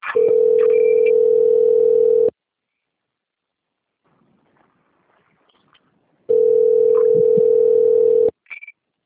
Ringback tone (North America), 440 + 480 Hz.